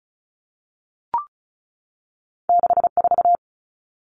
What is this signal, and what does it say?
Morse code, heard as a keyed tone, about 36 wpm, 700 Hz. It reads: 64
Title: Common Cut Numbers - 2-Digit 35wpm